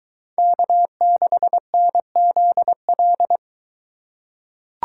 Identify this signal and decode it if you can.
Morse code, heard as a keyed tone, about 23 wpm, 700 Hz. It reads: K6NZL